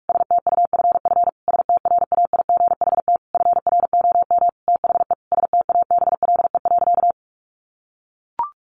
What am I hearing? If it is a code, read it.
Morse code, heard as a keyed tone, about 44 wpm, 700 Hz. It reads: STUFF STRAIGHT FROM THE STABLE.
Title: Sentences from Top 1500 Words 45wpm